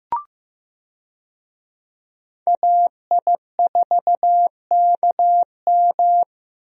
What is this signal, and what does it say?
Morse code, heard as a keyed tone, about 16 wpm, 700 Hz. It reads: AI4KM
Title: USA Call Signs - 5 Characters - 15wpm